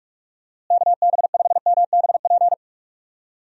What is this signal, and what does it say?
Morse code, heard as a keyed tone, about 46 wpm, 700 Hz. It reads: KB5MBP
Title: USA Call Signs - 6 Characters - Repeated 2x - 45wpm